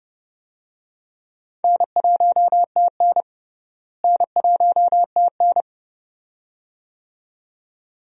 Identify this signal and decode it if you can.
Morse code, heard as a keyed tone, about 30 wpm, 700 Hz. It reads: N1TN N1TN
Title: USA Call Signs - 4 Characters - Repeated 2x - 30wpm